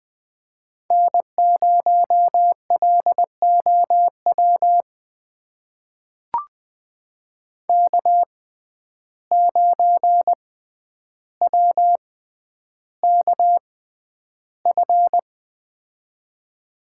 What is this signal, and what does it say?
Morse code, heard as a keyed tone, about 20 wpm, spaced as for 10 wpm, 700 Hz. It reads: N0LOW K9WKF
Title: USA Call Signs - 5 Characters - 6x Character Spacing - 20wpm